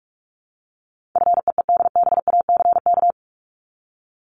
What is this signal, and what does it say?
Morse code, heard as a keyed tone, about 44 wpm, 700 Hz. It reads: FEEDBACK
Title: Top 1501-1750 Words 45wpm